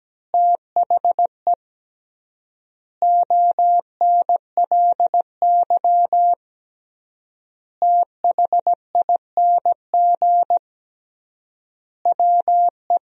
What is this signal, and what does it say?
Morse code, heard as a keyed tone, about 17 wpm, 700 Hz. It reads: THE ONLY THING WE
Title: 3x Word-Spacing - Sentences from Top 200 Words - 17wpm